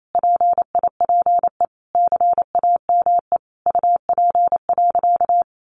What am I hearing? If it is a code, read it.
Morse code, heard as a keyed tone, about 28 wpm, 700 Hz. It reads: PIPE CAME UP.
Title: Sentences from Top 1500 Words 28wpm